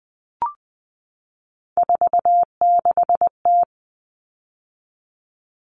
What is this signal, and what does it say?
Morse code, heard as a keyed tone, about 20 wpm, 700 Hz. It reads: 46T
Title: All Cut Numbers - 3-Digit 20wpm